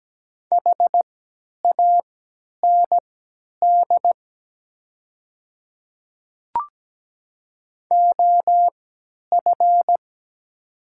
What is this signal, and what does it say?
Morse code, heard as a keyed tone, about 17 wpm, spaced as for 10 wpm, 700 Hz. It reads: HAND OF
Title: 3x Character Spacing - Top 200 Words - 17wpm